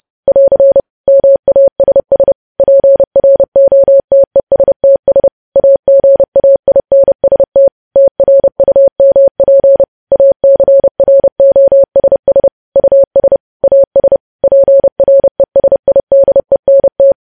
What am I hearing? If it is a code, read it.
Morse code, heard as a keyed tone, about 30 wpm, 550 Hz. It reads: <AR> MASS PROTESTS AGAINST TRUMP ACROSS US AS PRESIDENT
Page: news headlines in morse code at 30 wpm